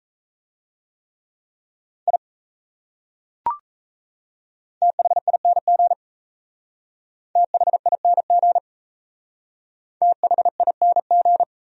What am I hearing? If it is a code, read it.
Morse code, heard as a keyed tone, about 34 wpm, 700 Hz. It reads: I THING THING THING
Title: Speed-Racing Top 100 Words 28wpm